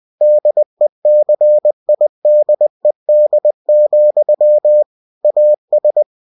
Morse code, heard as a keyed tone, about 20 wpm, 600 Hz. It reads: DECIDED, AS
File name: cw-books / winnie-the-pooh / winnie-20wpm-page-0004